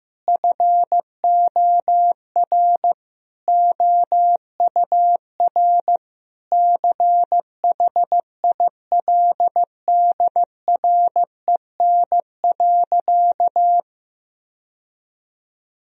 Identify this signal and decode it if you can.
Morse code, heard as a keyed tone, about 15 wpm, 700 Hz. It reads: FOR OUR CHILDREN.
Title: Sentences from Top 1000 Words - Review Part 4 - 15wpm